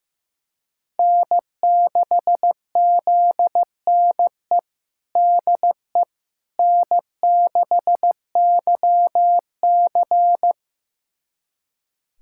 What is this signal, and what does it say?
Morse code, heard as a keyed tone, about 15 wpm, 700 Hz. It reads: N6ZNE DE N6YC
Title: USA Callsign DE Callsign 15wpm